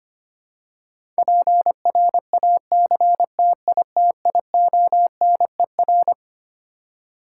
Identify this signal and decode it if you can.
Morse code, heard as a keyed tone, about 25 wpm, 700 Hz. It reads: PRACTITIONER
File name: Twelve Letter Words - 25wpm